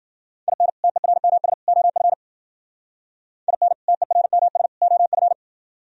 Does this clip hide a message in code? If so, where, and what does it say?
Yes: Morse code from 0.5 to 5.3 s, heard as a keyed tone, about 52 wpm, 700 Hz. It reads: IN TERMS OF IN TERMS OF